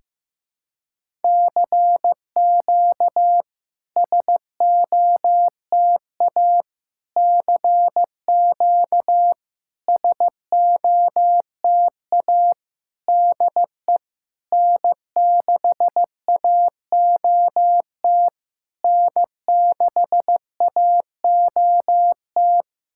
Morse code, heard as a keyed tone, about 15 wpm, 700 Hz. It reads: CQ SOTA CQ SOTA DE N6AOT N6AOT
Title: SOTA QSO Practice 15wpm